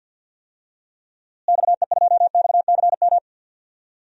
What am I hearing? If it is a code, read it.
Morse code, heard as a keyed tone, about 47 wpm, 700 Hz. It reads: XE2XCM
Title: DXCC-100 Call Signs - 6 Characters - Repeated 3x - 50wpm